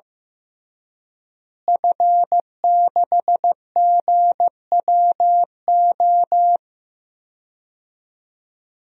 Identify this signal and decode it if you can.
Morse code, heard as a keyed tone, about 15 wpm, 700 Hz. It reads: F6GWO
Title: DXCC-100 Call Signs - 5 Characters - Repeated 3x - 15wpm